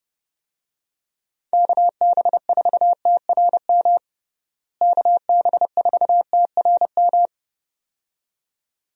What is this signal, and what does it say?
Morse code, heard as a keyed tone, about 32 wpm, 700 Hz. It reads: KB4TRM KB4TRM
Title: USA Call Signs - 6 Characters - Repeated 2x - 30wpm